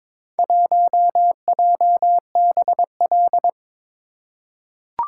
Morse code, heard as a keyed tone, about 22 wpm, 700 Hz. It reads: 1JBL
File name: practice / 22wpm / Four Letter-Number Combinations - 22wpm